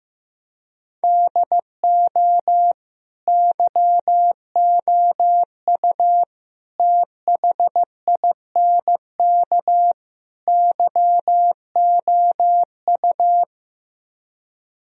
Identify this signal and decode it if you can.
Morse code, heard as a keyed tone, about 15 wpm, 700 Hz. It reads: DO YOU THINK YOU
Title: Speed-Racing - Sets of 4 Words 15wpm